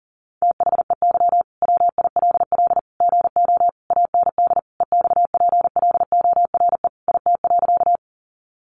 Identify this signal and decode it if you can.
Morse code, heard as a keyed tone, about 40 wpm, 700 Hz. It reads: THEY WILL GO AND EXPLORE IT.
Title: Sentences from Top 1100 Words 40wpm